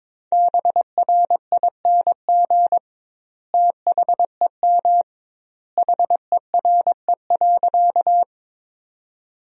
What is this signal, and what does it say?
Morse code, heard as a keyed tone, about 22 wpm, 700 Hz. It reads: BRING THEM HERE.
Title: 2x Word-Spacing - Sentences from Top 500 Words - 22wpm